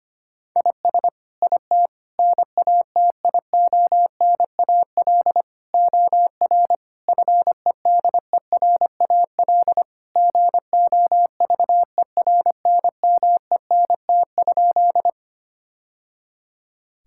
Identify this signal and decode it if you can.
Morse code, heard as a keyed tone, about 25 wpm, 700 Hz. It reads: IS IT NATIONAL OR FEDERAL GOVERNMENT?
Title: Sentences from Top 1000 Words - Review Part 5 - 25wpm